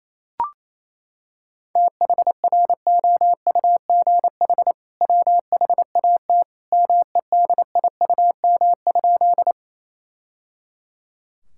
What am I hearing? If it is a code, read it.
Morse code, heard as a keyed tone, about 27 wpm, 700 Hz. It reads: THROUGH WHAT MEDIUM?
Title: Sentences from Top 1000 Words - Review Part 6 - 28wpm